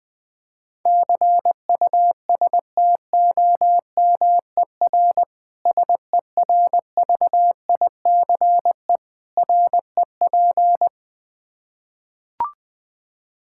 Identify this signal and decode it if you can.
Morse code, heard as a keyed tone, about 20 wpm, 700 Hz. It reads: CUSTOMER SERVICE REP
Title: K1USN SST QSO Practice 20wpm